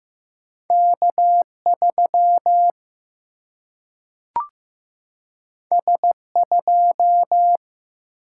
Morse code, heard as a keyed tone, about 15 wpm, 700 Hz. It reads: K 3 S 2
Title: Single-Letter Single-Number 15wpm